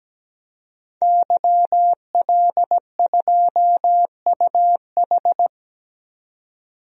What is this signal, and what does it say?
Morse code, heard as a keyed tone, about 17 wpm, 700 Hz. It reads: YL2UH
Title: DXCC-100 Call Signs - 5 Characters - 17wpm